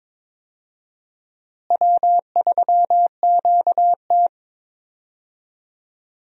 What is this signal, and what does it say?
Morse code, heard as a keyed tone, about 21 wpm, 700 Hz. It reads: W3QT
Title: USA Call Signs - 4 Characters - Repeated 2x - 22wpm